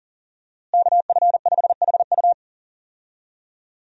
Morse code, heard as a keyed tone, about 37 wpm, 700 Hz. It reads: KF5HU
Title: USA Call Signs - 5 Characters - Repeated 2x - 40wpm